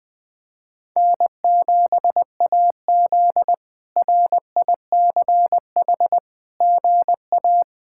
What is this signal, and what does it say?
Morse code, heard as a keyed tone, about 20 wpm, 700 Hz. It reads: N7AZ RICH GA
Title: K1USN SST QSO Practice - 20wpm